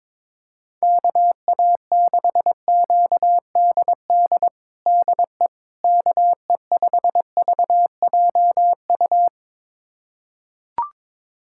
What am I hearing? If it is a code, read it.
Morse code, heard as a keyed tone, about 22 wpm, 700 Hz. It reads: KA6QDD DE KE5VJU
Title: USA Callsign DE Callsign 22wpm